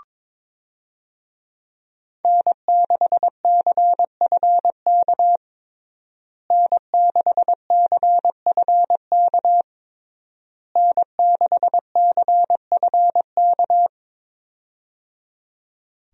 Morse code, heard as a keyed tone, about 21 wpm, 700 Hz. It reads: N6CFK N6CFK N6CFK
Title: USA Call Signs - 5 Characters - Repeated 3x - 22wpm